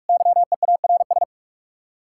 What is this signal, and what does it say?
Morse code, heard as a keyed tone, about 59 wpm, spaced as for 52 wpm, 700 Hz. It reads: YEARS
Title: Speed-Racing Common QSO Phrases Encore - 45wpm